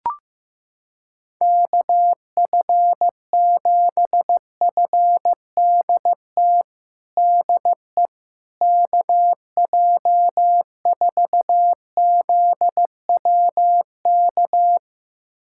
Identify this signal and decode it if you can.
Morse code, heard as a keyed tone, about 15 wpm, 700 Hz. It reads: KF7FDT DE KJ4ZWK
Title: USA Callsign DE Callsign - 6 Characters - 15wpm